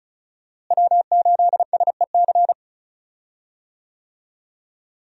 Morse code, heard as a keyed tone, about 34 wpm, 700 Hz. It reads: W8SEC
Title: USA Call Signs - 5 Characters - Repeated 2x - 35wpm